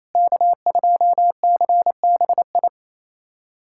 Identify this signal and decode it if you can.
Morse code, heard as a keyed tone, about 28 wpm, 700 Hz. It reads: K2CBI
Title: USA Callsign DE Callsign - 5 Characters - 28wpm